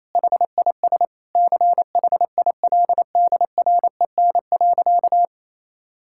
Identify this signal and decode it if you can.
Morse code, heard as a keyed tone, about 27 wpm, 700 Hz. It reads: HIS CHILDREN.
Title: Sentences from Top 1600 Words 28wpm